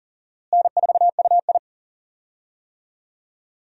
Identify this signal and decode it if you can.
Morse code, heard as a keyed tone, about 40 wpm, 700 Hz. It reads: N4UI
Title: USA Call Signs - 4 Characters - Repeated 2x - 40wpm